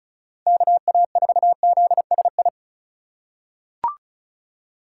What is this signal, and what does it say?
Morse code, heard as a keyed tone, about 36 wpm, 700 Hz. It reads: KA4ZSI
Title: USA Call Signs - 6 Characters - 4x Character Spacing - 35wpm